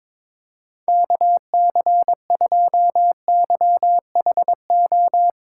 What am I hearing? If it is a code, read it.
Morse code, heard as a keyed tone, about 22 wpm, 700 Hz. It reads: KC2YHO
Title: USA Callsign DE Callsign - 6 Characters - 22wpm